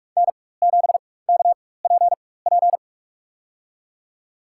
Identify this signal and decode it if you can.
Morse code, heard as a keyed tone, about 42 wpm, spaced as for 21 wpm, 700 Hz. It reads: N7KPP
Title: USA Call Signs - 5 Characters - 4x Character Spacing - 45wpm